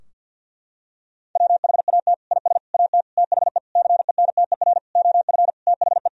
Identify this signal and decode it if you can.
Morse code, heard as a keyed tone, about 47 wpm, 700 Hz. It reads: WHAT IS AT THE CENTER OF THE
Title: Sentences from Top 1000 Words - Review Part 2 - 50wpm